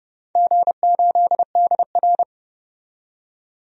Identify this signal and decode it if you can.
Morse code, heard as a keyed tone, about 30 wpm, 700 Hz. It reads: G8DR
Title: DXCC-100 Call Signs - 4 Characters - 30wpm